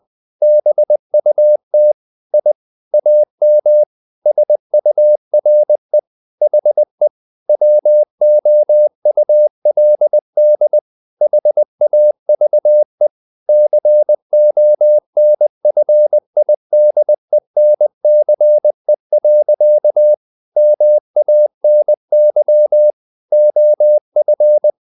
Morse code, heard as a keyed tone, about 20 wpm, 600 Hz. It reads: BUT I AM SURE HE WOULD HAVE CONFIDENCE. MANY OF
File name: the old man and the sea part 2 20wpm